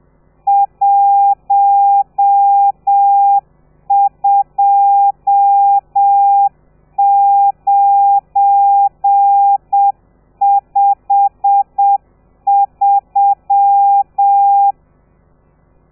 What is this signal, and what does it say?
Morse code, heard as a keyed tone, about 7 wpm, 800 Hz. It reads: 12953
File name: telegrafia / i2tfj / Lez12 30